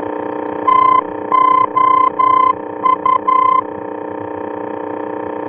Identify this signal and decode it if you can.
Morse code, heard as a keyed tone, about 11 wpm, 1030 Hz, with noise. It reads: TOU
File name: france / mx / TOU-F-VOR